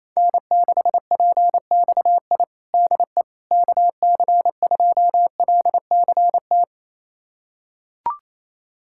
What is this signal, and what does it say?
Morse code, heard as a keyed tone, about 28 wpm, 700 Hz. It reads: N6PXI DE KC2LCT
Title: USA Callsign DE Callsign 28wpm